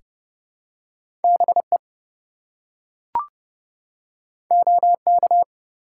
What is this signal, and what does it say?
Morse code, heard as a keyed tone, about 30 wpm, 700 Hz. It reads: BE OK
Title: Two Letter Words - 30wpm